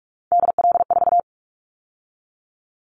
Morse code, heard as a keyed tone, about 46 wpm, 700 Hz. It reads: DL4
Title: Callsign Prefixes for Top 50 DXCC - 45wpm